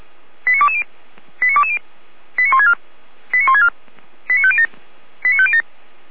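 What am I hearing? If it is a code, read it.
Five-tone selective call (ZVEI), heard as quick runs of five tones, tones of about 70 ms. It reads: 89220 89220 89164 89164 89507 89507
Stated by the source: ZVEI five-tone selective call received by SDR.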